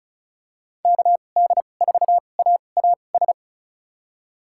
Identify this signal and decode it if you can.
Morse code, heard as a keyed tone, about 36 wpm, spaced as for 25 wpm, 700 Hz. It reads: KD4AAS
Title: USA Call Signs - 2x Character Spacing - 35wpm